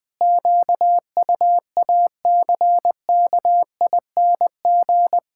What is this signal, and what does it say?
Morse code, heard as a keyed tone, about 20 wpm, 700 Hz. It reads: QUACKING
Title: Pangram Challenge v2 - 20wpm